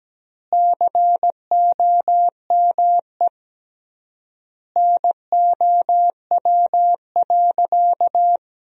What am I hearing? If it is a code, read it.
Morse code, heard as a keyed tone, about 17 wpm, 700 Hz. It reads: COME NOW.
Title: 3x Word-Spacing - Sentences from Top 500 Words - 17wpm